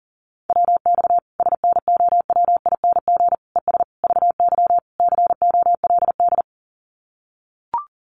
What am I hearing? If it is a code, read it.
Morse code, heard as a keyed tone, about 40 wpm, 700 Hz. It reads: WX SNOWING ES VY COLD
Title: Common QSO Phrases 40wpm